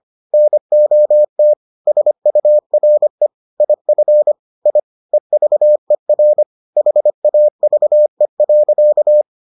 Morse code, heard as a keyed tone, about 25 wpm, 600 Hz. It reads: NOT SURE IF I EVER HAVE.